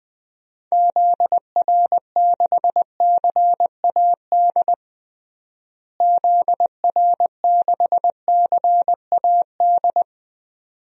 Morse code, heard as a keyed tone, about 20 wpm, 700 Hz. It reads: ZR6CAD ZR6CAD
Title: DXCC-100 Call Signs - 6 Characters - Repeated 2x - 20wpm